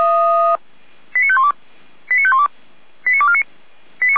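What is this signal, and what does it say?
Five-tone selective call (ZVEI), heard as quick runs of five tones, tones of about 70 ms. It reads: 89512 89512 89249
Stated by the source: ZVEI five-tone selective call received by SDR, from fire brigade Feuerwehr Orlamünde.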